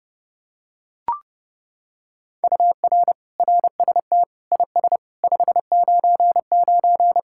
Morse code, heard as a keyed tone, about 29 wpm, 700 Hz. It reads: UR RST IS 599